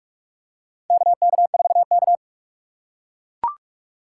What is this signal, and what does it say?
Morse code, heard as a keyed tone, about 41 wpm, 700 Hz. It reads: KK4K
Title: USA Call Signs - 4 Characters - 3x Character Spacing - 45wpm